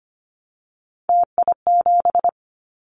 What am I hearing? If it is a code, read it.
Morse code, heard as a keyed tone, about 25 wpm, 700 Hz. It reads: TI7
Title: Callsign Prefixes for Top 100 DXCC - 25wpm